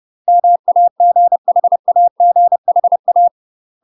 Morse code, heard as a keyed tone, about 30 wpm, 700 Hz. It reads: MAGHAGHA
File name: UK MASTER QSOs TRAINING / 30WPM / UK MASTER QSOs 0014